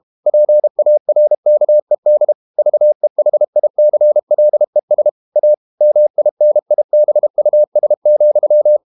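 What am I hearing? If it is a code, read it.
Morse code, heard as a keyed tone, about 32 wpm, 600 Hz. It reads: PARKED VEHICLES A MINIBUS,
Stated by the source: Morse code practice at 32 wpm from CW med Gnister, reading Ken Follett's novel Never in English.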